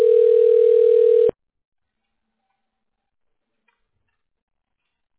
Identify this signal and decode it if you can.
Ringback tone (North America), 440 + 480 Hz.